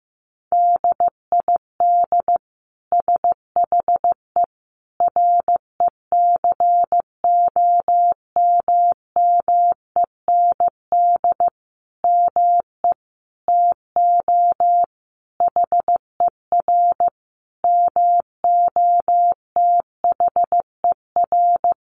Morse code, heard as a keyed tone, about 15 wpm, 700 Hz. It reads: DID SHE RECOMMEND ME TO HER MOTHER
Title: Sentences from Top 1200 Words 15wpm